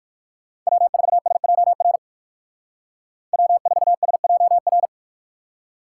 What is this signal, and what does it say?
Morse code, heard as a keyed tone, about 58 wpm, 700 Hz. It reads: W4SJR W4SJR